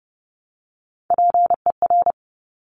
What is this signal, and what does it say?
Morse code, heard as a keyed tone, about 32 wpm, 690 Hz. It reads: PER
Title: Top Three Letter Combinations in Words 30wpm